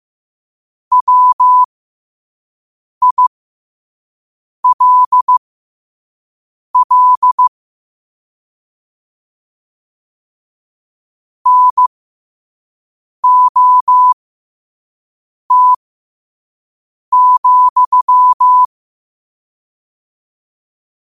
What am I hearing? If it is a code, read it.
Morse code, heard as a keyed tone, about 15 wpm, spaced as for 5 wpm, 1000 Hz. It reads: WILL NOT,
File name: qotdfiles / 5wpm